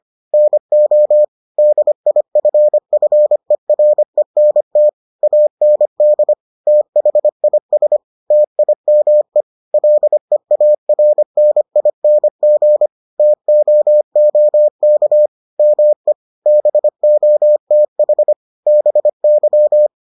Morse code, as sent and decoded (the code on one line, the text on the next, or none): -. --- / -.. .. ..-. ..-. .-. -. - / .- -. -.. / - .... .. ... / - .. -- . / .-.. .- .-. -. .. -. --. / - --- --- -.- / -- . / -... --- - .... / -... -.--
NO DIFFERENT AND THIS TIME LEARNING TOOK ME BOTH BY